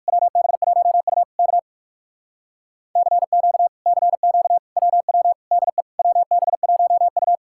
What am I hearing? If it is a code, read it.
Morse code, heard as a keyed tone, about 66 wpm, spaced as for 56 wpm, 700 Hz. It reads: WB1U K CQ CQ WW DE WB1U
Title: Speed-Racing Common QSO Phrases - 45wpm